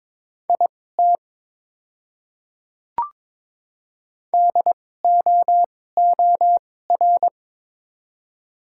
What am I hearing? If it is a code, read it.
Morse code, heard as a keyed tone, about 22 wpm, spaced as for 16 wpm, 700 Hz. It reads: IT DOOR
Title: 2x Character Spacing - Top 300 Words - 22wpm